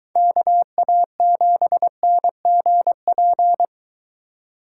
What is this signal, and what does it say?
Morse code, heard as a keyed tone, about 22 wpm, 700 Hz. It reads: KA7NGP